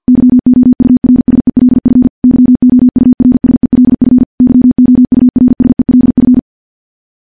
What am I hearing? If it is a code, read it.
Morse code, heard as a keyed tone, about 50 wpm, 250 Hz. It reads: YOURSELF YOURSELF YOURSELF
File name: morse / morse-training-50WPM